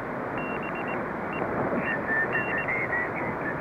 Morse code, heard as a keyed tone, about 19 wpm, spaced as for 13 wpm, 2750 Hz, with noise. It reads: BEEN
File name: Ham 2 / 2006.03.04.VE1JF.7060.7.khz.lsb.411.pm.est